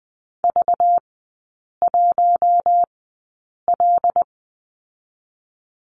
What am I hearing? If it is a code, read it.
Morse code, heard as a keyed tone, about 20 wpm, spaced as for 8 wpm, 700 Hz. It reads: V1L
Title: 205 Lesson 33e - Letter j - Three Characters 20wpm